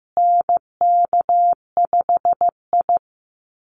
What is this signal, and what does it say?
Morse code, heard as a keyed tone, about 15 wpm, 700 Hz. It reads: NK5I
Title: USA Call Signs Repeated 2x 15wpm